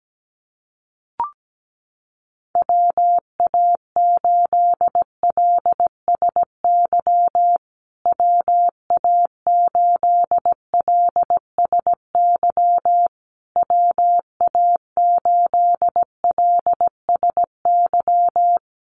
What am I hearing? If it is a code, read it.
Morse code, heard as a keyed tone, about 17 wpm, 700 Hz. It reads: WA8LSY WA8LSY WA8LSY
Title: USA Call Signs Repeated 3x 17wpm